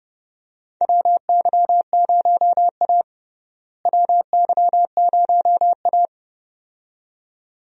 Morse code, heard as a keyed tone, about 30 wpm, 700 Hz. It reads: WY0A WY0A
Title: USA Call Signs - 4 Characters - Repeated 2x - 30wpm